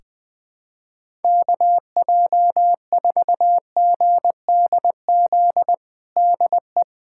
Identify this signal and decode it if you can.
Morse code, heard as a keyed tone, about 21 wpm, 700 Hz. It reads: KJ4GDZ DE
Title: USA Callsign DE Callsign 20wpm